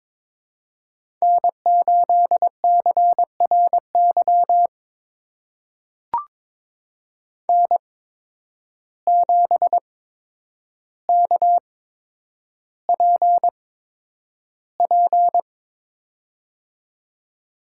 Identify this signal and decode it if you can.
Morse code, heard as a keyed tone, about 22 wpm, spaced as for 9 wpm, 700 Hz. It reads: N8CRY N7KPP
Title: USA Call Signs - 5 Characters - 8x Character Spacing - 22wpm